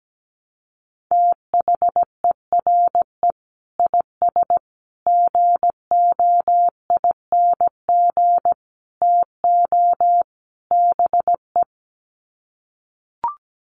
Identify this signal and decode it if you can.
Morse code, heard as a keyed tone, about 17 wpm, 700 Hz. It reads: THERE IS GOING TO BE
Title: Sets of 5 Words Encore 17wpm